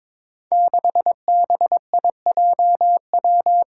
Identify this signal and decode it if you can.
Morse code, heard as a keyed tone, about 22 wpm, 700 Hz. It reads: 6BIJW
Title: Five Challenging Letter-Number Combinations 22wpm